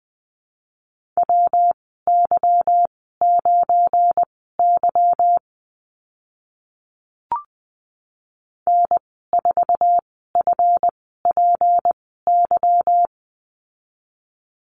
Morse code, heard as a keyed tone, about 20 wpm, spaced as for 15 wpm, 700 Hz. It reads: WY9Y N4FPY
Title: 186 Lesson 30g - Letter b - Call Signs 20wpm